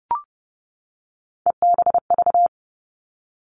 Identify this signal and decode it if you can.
Morse code, heard as a keyed tone, about 30 wpm, 700 Hz. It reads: EBV
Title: Three Challenging Letter-Number Combinations 30wpm